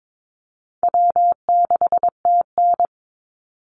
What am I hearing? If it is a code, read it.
Morse code, heard as a keyed tone, about 22 wpm, 700 Hz. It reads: W6TN
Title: Easy Call Signs - 22wpm